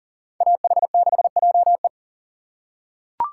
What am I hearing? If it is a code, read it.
Morse code, heard as a keyed tone, about 40 wpm, 700 Hz. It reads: AH6JE
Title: USA Call Signs - 5 Characters - 3x Character Spacing - 40wpm